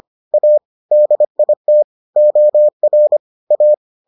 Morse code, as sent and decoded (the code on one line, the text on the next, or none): .- / -.. .. - / --- .-. / .-
A DIT OR A